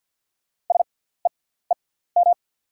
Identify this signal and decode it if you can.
Morse code, heard as a keyed tone, about 50 wpm, spaced as for 17 wpm, 690 Hz. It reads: SEEM
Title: 6x Character Spacing - Top 300 Words - 50wpm